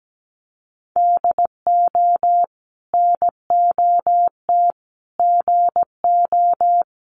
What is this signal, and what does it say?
Morse code, heard as a keyed tone, about 17 wpm, 700 Hz. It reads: DO NOT GO
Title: Sentences from Top 1500 Words 17wpm